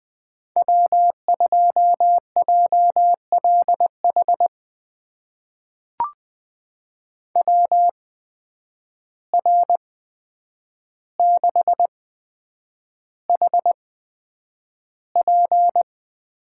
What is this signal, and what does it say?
Morse code, heard as a keyed tone, about 20 wpm, spaced as for 9 wpm, 700 Hz. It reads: W2JLH WR6HP
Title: USA Call Signs - 5 Characters - 8x Character Spacing - 20wpm